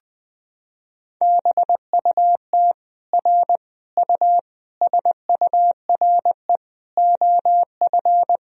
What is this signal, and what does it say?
Morse code, heard as a keyed tone, about 21 wpm, 700 Hz. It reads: BUT R U SURE OF
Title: Sentences from Top 300 Words and Abbreviations 20wpm